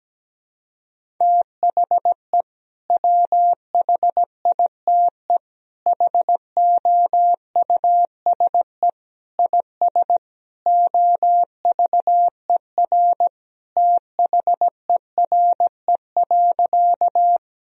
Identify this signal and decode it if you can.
Morse code, heard as a keyed tone, about 17 wpm, 700 Hz. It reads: THE WHITE HOUSE IS OVER THERE.
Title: Sentences from Top 1000 Words - Review Part 1 - 17wpm